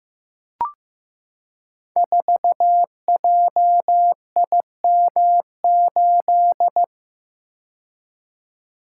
Morse code, heard as a keyed tone, about 15 wpm, 700 Hz. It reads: 4JIM8
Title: Code Groups - 5 Letter:Numbers - 15wpm